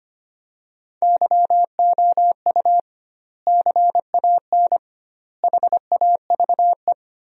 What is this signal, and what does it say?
Morse code, heard as a keyed tone, about 25 wpm, 700 Hz. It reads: YOU CAN HAVE
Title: 2x Word-Spacing - Sentences from Top 100 Words - 25wpm